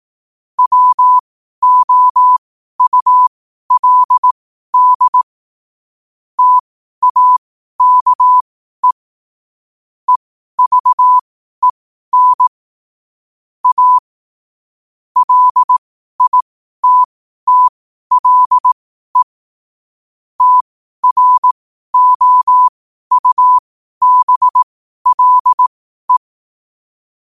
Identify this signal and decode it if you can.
Morse code, heard as a keyed tone, about 18 wpm, spaced as for 12 wpm, 1000 Hz. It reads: WOULD TAKE EVEN A LITTLE TROUBLE
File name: qotdfiles / 12wpm